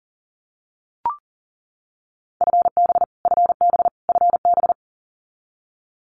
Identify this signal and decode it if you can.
Morse code, heard as a keyed tone, about 40 wpm, 700 Hz. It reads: FB FB FB
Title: Abbreviations Repeated 3x - 40wpm